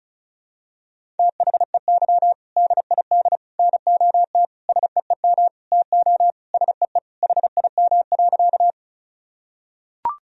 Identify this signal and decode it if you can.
Morse code, heard as a keyed tone, about 36 wpm, 700 Hz. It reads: THEY DID NOT SEEM TO SEE HIM.
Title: Sentences from Top 1000 Words - Review Part 1 - 35wpm